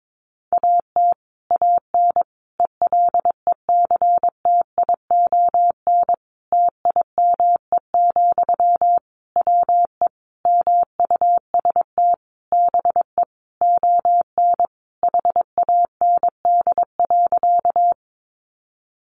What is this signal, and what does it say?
Morse code, heard as a keyed tone, about 22 wpm, 700 Hz. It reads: AT AN ELECTION TIME, WE MUST BE ON HAND.
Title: Sentences from Top 1500 Words 22wpm